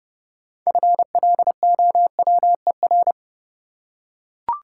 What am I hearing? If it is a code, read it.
Morse code, heard as a keyed tone, about 29 wpm, 700 Hz. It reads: FLOWER
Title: Six Letter Words - 30wpm